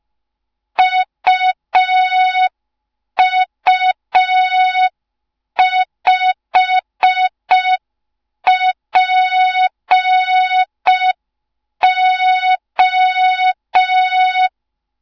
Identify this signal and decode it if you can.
Morse code, heard as a keyed tone, about 5 wpm, 3700 Hz. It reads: UU5PO
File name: Morze-HA2MN / hang / 07 GYAKORLAT